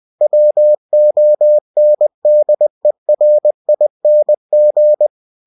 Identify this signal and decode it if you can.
Morse code, heard as a keyed tone, about 20 wpm, 600 Hz. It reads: WONDERING
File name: cw-books / winnie-the-pooh / winnie-20wpm-page-0033